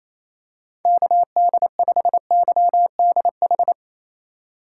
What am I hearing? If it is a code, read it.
Morse code, heard as a keyed tone, about 28 wpm, 700 Hz. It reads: KD5YDH
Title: USA Call Signs - 6 Characters - 8x Character Spacing - 28wpm